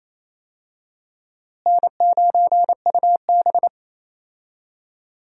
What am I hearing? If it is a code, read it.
Morse code, heard as a keyed tone, about 28 wpm, 700 Hz. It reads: N9UB
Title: USA Call Signs - 4 Characters - Repeated 3x - 28wpm